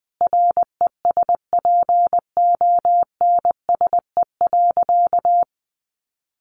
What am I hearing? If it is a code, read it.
Morse code, heard as a keyed tone, about 20 wpm, 700 Hz. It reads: RESPONSE.
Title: Sentences from Top 1400 Words 20wpm